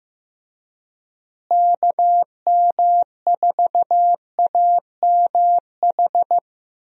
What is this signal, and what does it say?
Morse code, heard as a keyed tone, about 15 wpm, 700 Hz. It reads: KM4AMH
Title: USA Call Signs - 6 Characters - Repeated 3x - 15wpm